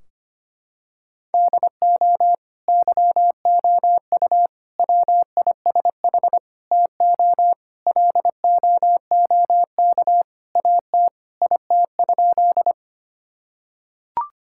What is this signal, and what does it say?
Morse code, heard as a keyed tone, about 25 wpm, 700 Hz. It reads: DO YOU WISH TO LOOK AT IT?
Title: Sentences from Top 1600 Words 25wpm